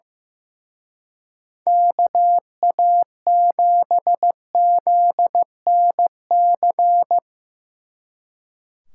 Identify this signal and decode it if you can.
Morse code, heard as a keyed tone, about 17 wpm, 700 Hz. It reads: KA7ZNC
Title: Speed Racing - USA 2x3 Call Signs 15wpm